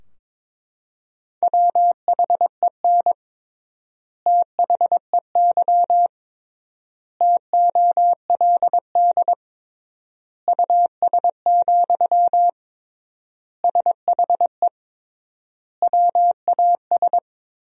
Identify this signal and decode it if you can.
Morse code, heard as a keyed tone, about 22 wpm, 700 Hz. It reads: WHEN THEY TOLD US, SHE WAS
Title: 3x Word-Spacing - Sentences from Top 400 Words - 22wpm